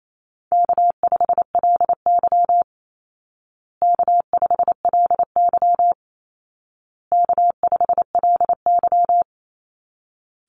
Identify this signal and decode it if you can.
Morse code, heard as a keyed tone, about 28 wpm, 700 Hz. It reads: K5LY K5LY K5LY
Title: Easy Call Signs Repeated 3x - 28wpm